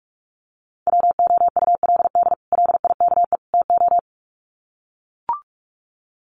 Morse code, heard as a keyed tone, about 44 wpm, 700 Hz. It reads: WOULD LIKE TO
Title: Sets of 3 Words 45wpm